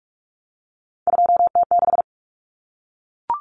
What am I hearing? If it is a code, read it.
Morse code, heard as a keyed tone, about 44 wpm, 700 Hz. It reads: 2T6
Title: Common Cut Numbers - 3-Digit 45wpm